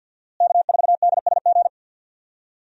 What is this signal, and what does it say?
Morse code, heard as a keyed tone, about 51 wpm, 700 Hz. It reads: K4DSG
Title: USA Call Signs - 5 Characters - Repeated 2x - 50wpm